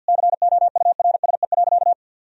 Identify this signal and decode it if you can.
Morse code, heard as a keyed tone, about 51 wpm, 700 Hz. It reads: COURSE.
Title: Sentences from Top 1000 Words - Review Part 1 - 50wpm